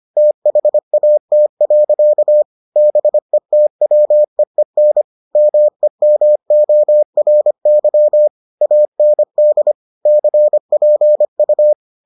Morse code, heard as a keyed tone, about 25 wpm, 600 Hz. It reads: THAT. BETWEEN MEMORY AND CPU